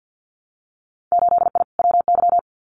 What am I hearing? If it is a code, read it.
Morse code, heard as a keyed tone, about 50 wpm, 700 Hz. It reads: 8I WY
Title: Winter Field Day Exchanges Encore 50wpm